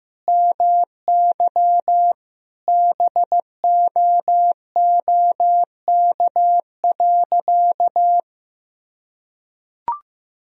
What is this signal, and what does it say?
Morse code, heard as a keyed tone, about 15 wpm, 700 Hz. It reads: MY BOOK.
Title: Sentences from Top 1000 Words - Review Part 1 - 15wpm